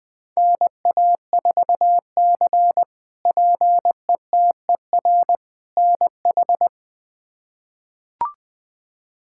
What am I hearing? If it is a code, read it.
Morse code, heard as a keyed tone, about 21 wpm, 700 Hz. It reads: NA4C PETER NH
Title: K1USN SST QSO Practice - 20wpm